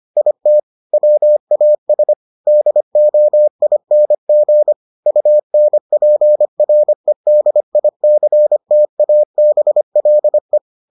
Morse code, heard as a keyed tone, about 25 wpm, 600 Hz. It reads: IT WAS DOING UNPREDICTABLE